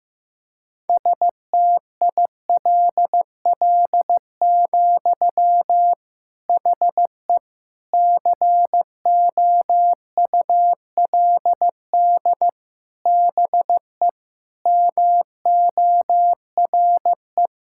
Morse code, heard as a keyed tone, about 15 wpm, 700 Hz. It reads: STILL, HE COULD BE MORE
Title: Sentences from Top 1000 Words - Review Part 5 - 15wpm